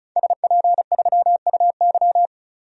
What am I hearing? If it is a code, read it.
Morse code, heard as a keyed tone, about 34 wpm, 700 Hz. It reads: SP3UY
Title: DXCC-100 Call Signs - 5 Characters - 35wpm